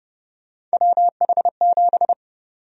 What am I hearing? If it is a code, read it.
Morse code, heard as a keyed tone, about 30 wpm, 700 Hz. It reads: WH7
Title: USA Call Sign Prefixes - Repeated 2x - 30wpm